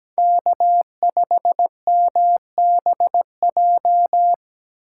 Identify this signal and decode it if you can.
Morse code, heard as a keyed tone, about 17 wpm, 700 Hz. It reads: K5MBJ
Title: USA Call Signs - 5 Characters - 3x Character Spacing - 17wpm